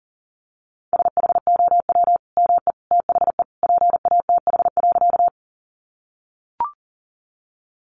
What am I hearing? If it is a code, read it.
Morse code, heard as a keyed tone, about 37 wpm, 700 Hz. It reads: SHOW ME THE PATH.
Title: Sentences from Top 1400 Words 40wpm